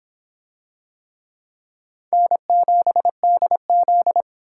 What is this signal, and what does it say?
Morse code, heard as a keyed tone, about 25 wpm, 700 Hz. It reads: N7DZ